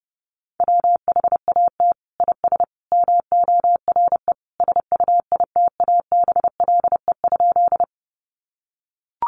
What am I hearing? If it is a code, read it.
Morse code, heard as a keyed tone, about 30 wpm, 700 Hz. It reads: WHAT IS MORE SUITABLE?
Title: Sentences from Top 1400 Words 30wpm